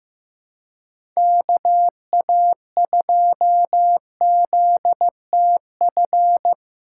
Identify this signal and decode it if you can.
Morse code, heard as a keyed tone, about 15 wpm, 700 Hz. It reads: KA2ZTF
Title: USA Call Signs - 6 Characters - 3x Character Spacing - 15wpm